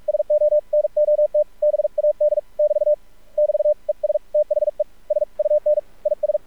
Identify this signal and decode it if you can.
Morse code, heard as a keyed tone, about 46 wpm, 600 Hz. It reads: SO NOT BAD <BT> <BT> ES THE SUN IS
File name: archived / CRSnet